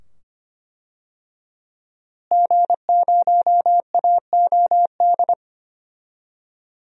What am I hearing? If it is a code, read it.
Morse code, heard as a keyed tone, about 25 wpm, 700 Hz. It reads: G0AOD
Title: DXCC-100 Call Signs - 5 Characters - Repeated 2x - 25wpm